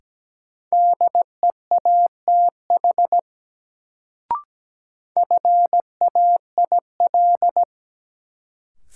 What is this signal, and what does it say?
Morse code, heard as a keyed tone, about 17 wpm, 700 Hz. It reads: DEATH FAIL
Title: Sentences from Top 1100 Words - New Word Drill - 17wpm